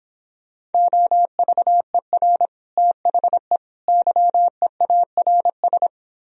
Morse code, heard as a keyed tone, about 31 wpm, 700 Hz. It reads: OVER THE YEARS